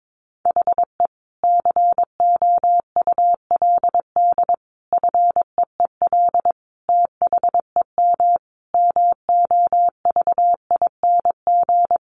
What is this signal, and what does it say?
Morse code, heard as a keyed tone, about 22 wpm, 700 Hz. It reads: HE COULD FEEL THEM MOVING
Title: A Sound Of Thunder 22wpm